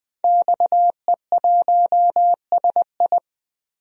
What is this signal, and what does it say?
Morse code, heard as a keyed tone, about 20 wpm, 700 Hz. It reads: XE1SI
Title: DXCC-100 Call Signs - 5 Characters - Repeated 3x - 20wpm